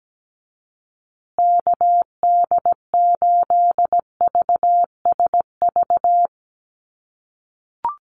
Morse code, heard as a keyed tone, about 17 wpm, 700 Hz. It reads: KD8VSV
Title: USA Call Signs Repeated 3x 17wpm